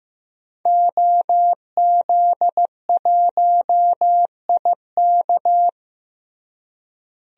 Morse code, heard as a keyed tone, about 15 wpm, 700 Hz. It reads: OZ1IK
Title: DXCC-100 Call Signs - 5 Characters - Repeated 2x - 15wpm